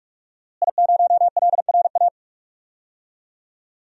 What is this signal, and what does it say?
Morse code, heard as a keyed tone, about 45 wpm, 700 Hz. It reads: I0LRA
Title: DXCC-100 Call Signs - 5 Characters - Repeated 3x - 45wpm